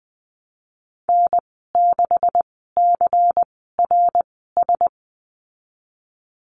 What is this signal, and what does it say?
Morse code, heard as a keyed tone, about 20 wpm, spaced as for 15 wpm, 700 Hz. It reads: N6CRS
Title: 173 Lesson 28g - Number 9 - Call Signs 20wpm